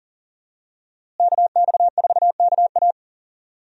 Morse code, heard as a keyed tone, about 40 wpm, 700 Hz. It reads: KX4KA
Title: USA Call Signs - 5 Characters - Repeated 2x - 40wpm